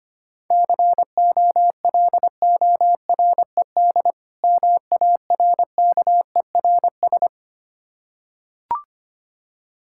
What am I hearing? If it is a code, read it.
Morse code, heard as a keyed tone, about 25 wpm, 690 Hz. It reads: COLORED MARKERS
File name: practice / 25wpm / Difficult Words with Hints - 25wpm